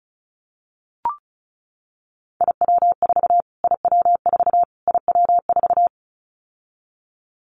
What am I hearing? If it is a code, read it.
Morse code, heard as a keyed tone, about 34 wpm, 700 Hz. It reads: IW4 IW4 IW4
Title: Callsign Prefixes for Top 25 DXCC - Repeated 3x - 35wpm